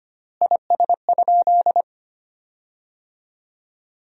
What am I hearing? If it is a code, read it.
Morse code, heard as a keyed tone, about 25 wpm, 700 Hz. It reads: IS?
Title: 4x Word-Spacing - Sentences from Top 100 Words - 25wpm